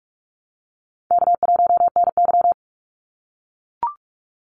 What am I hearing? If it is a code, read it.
Morse code, heard as a keyed tone, about 44 wpm, 700 Hz. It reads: K1NY
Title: Easy Call Signs - 45wpm